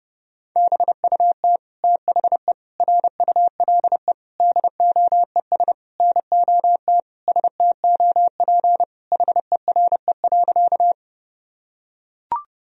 Morse code, heard as a keyed tone, about 28 wpm, 700 Hz. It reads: BUT THE RULE DOES NOT STOP HERE.
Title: Sentences from Top 1000 Words - Review Part 2 - 30wpm